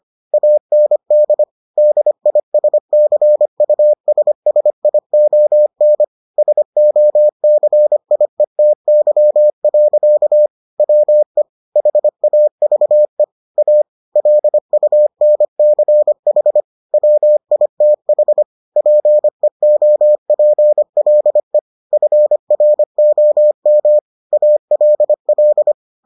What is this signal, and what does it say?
Morse code, heard as a keyed tone, about 25 wpm, 600 Hz. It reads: AND DISCUSSION SOCIETY. WE HAVE A LUNCH WITH PEOPLE FROM ALL